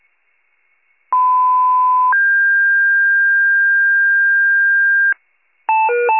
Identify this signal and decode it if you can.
Two-tone page, heard as a one-second tone, then a three-second tone, about 1010 then 1600 Hz. It is heard from 1.1 to 5.1 s.
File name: wcalert